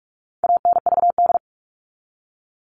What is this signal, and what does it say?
Morse code, heard as a keyed tone, about 44 wpm, 700 Hz. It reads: ANVD
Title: All Cut Numbers - 4-Digit 45wpm